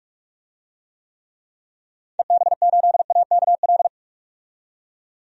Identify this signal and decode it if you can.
Morse code, heard as a keyed tone, about 41 wpm, 700 Hz. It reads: EB8AKL
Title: DXCC-100 Call Signs - 6 Characters - 45wpm